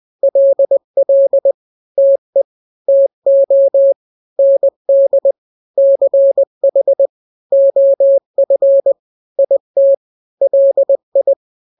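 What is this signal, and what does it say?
Morse code, heard as a keyed tone, about 20 wpm, 550 Hz. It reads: LL TE TO ND CH OF IT LI
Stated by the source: Morse drill of two-letter n-grams at 20/18 WPM (Farnsworth), from KK7RR.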